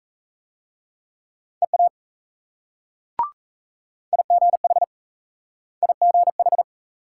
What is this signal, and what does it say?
Morse code, heard as a keyed tone, about 38 wpm, 700 Hz. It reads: EA IGH IGH